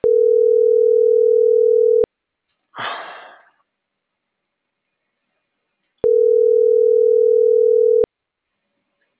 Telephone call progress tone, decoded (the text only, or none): ringback tone